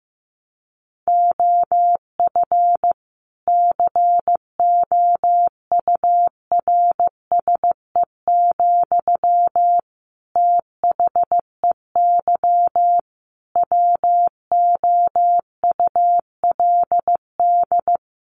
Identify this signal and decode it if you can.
Morse code, heard as a keyed tone, about 15 wpm, 700 Hz. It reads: OF COURSE, THEY WOULD
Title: Sentences from Top 1500 Words 15wpm